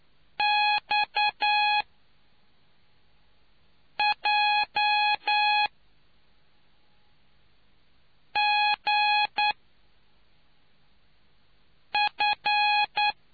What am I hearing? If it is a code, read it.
Morse code, heard as a keyed tone, about 9 wpm, 830 Hz. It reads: X J G F